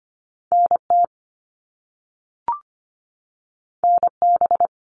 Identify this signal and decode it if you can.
Morse code, heard as a keyed tone, about 25 wpm, 700 Hz. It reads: NT NB
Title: Canadian Provinces and Territories 25wpm